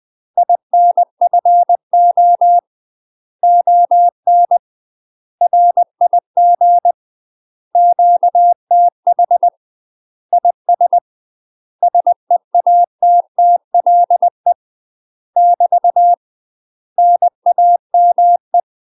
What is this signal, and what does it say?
Morse code, heard as a keyed tone, about 20 wpm, 700 Hz. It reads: INFO ON RIG QTH IS SEATTLE <BT> NAME
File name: UK MASTER QSOs TRAINING/20WPM